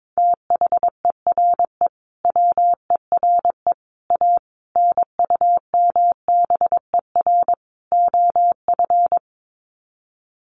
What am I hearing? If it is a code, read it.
Morse code, heard as a keyed tone, about 22 wpm, 700 Hz. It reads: THERE WERE A NUMBER OF
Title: Sets of 5 Words Encore2 22wpm